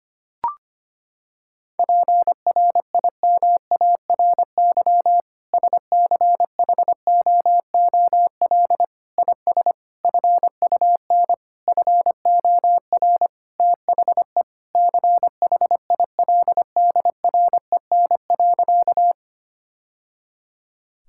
Morse code, heard as a keyed tone, about 25 wpm, 700 Hz. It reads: PRIMARY SCHOOL IS FUN FOR THE CHILDREN.
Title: Sentences from Top 1000 Words - Review Part 5 - 25wpm